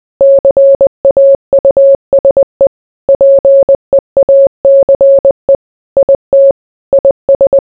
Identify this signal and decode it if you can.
Morse code, heard as a keyed tone, about 20 wpm, 550 Hz. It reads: CAUSE PEACE IT IS